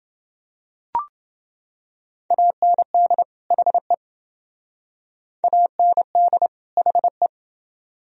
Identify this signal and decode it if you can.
Morse code, heard as a keyed tone, about 28 wpm, 700 Hz. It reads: AND HE AND HE